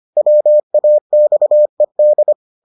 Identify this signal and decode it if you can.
Morse code, heard as a keyed tone, about 25 wpm, 600 Hz. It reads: WAXED